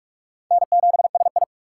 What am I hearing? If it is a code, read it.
Morse code, heard as a keyed tone, about 66 wpm, spaced as for 56 wpm, 700 Hz. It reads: N7SI
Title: Speed Racing - USA 1x2 Call Signs 45wpm